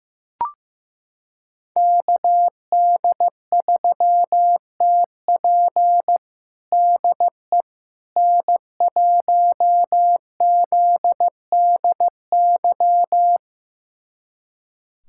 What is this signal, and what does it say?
Morse code, heard as a keyed tone, about 15 wpm, 700 Hz. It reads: KD3TP DE N1ZDY
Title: USA Callsign DE Callsign - 5 Characters - 15wpm